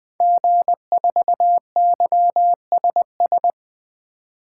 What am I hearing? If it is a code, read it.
Morse code, heard as a keyed tone, about 20 wpm, 700 Hz. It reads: G4YSS
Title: 2023 SOTA Activator Call Signs - 20wpm